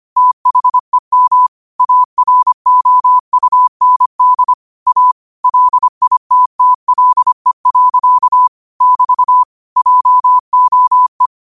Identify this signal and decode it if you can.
Morse code, heard as a keyed tone, about 25 wpm, 1000 Hz. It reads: THEM AROUND A LITTLE. <BT> JOE
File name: qotdfiles / 25wpm